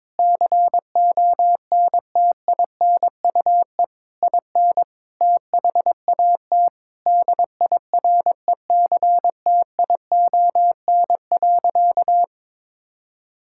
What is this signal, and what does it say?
Morse code, heard as a keyed tone, about 21 wpm, 700 Hz. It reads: CONTINUE IN THAT DIRECTION.
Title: Sentences from Top 1000 Words - Review Part 6 - 22wpm